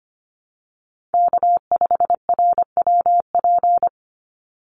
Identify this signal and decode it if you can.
Morse code, heard as a keyed tone, about 25 wpm, 700 Hz. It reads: K5RWP
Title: NA SOTA Chaser Callsigns 25wpm